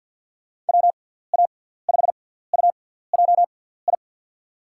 Morse code, heard as a keyed tone, about 53 wpm, spaced as for 17 wpm, 700 Hz. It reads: WA5UJI
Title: USA Call Signs - 6x Character Spacing - 50wpm